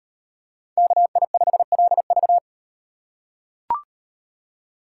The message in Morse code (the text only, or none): KI5LV